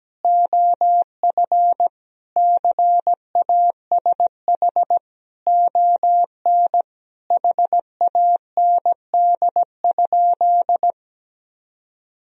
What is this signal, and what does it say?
Morse code, heard as a keyed tone, about 17 wpm, 700 Hz. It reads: OF CASH ON HAND?
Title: Sentences from Top 1000 Words - Review Part 4 - 17wpm